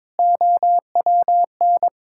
Morse code, heard as a keyed tone, about 22 wpm, 700 Hz. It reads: OWN
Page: morse-code-ninja-practice-22wpm